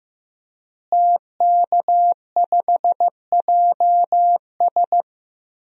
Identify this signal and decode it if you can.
Morse code, heard as a keyed tone, about 15 wpm, 700 Hz. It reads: TK5JS
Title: DXCC-100 Call Signs - 5 Characters - Repeated 3x - 15wpm